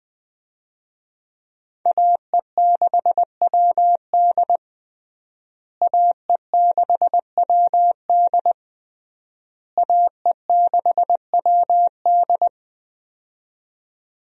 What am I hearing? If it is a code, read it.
Morse code, heard as a keyed tone, about 20 wpm, 700 Hz. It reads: AE6WD AE6WD AE6WD
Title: USA Call Signs - 5 Characters - Repeated 3x - 20wpm